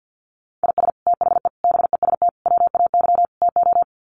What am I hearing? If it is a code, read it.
Morse code, heard as a keyed tone, about 50 wpm, 700 Hz. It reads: IS THE BEST WAY TO
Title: Sets of 5 Words Encore 50wpm